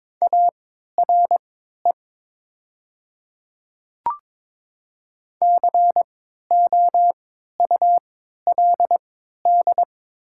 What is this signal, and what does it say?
Morse code, heard as a keyed tone, about 22 wpm, spaced as for 13 wpm, 700 Hz. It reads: ARE COULD
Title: 3x Character Spacing - Top 100 Words - 22wpm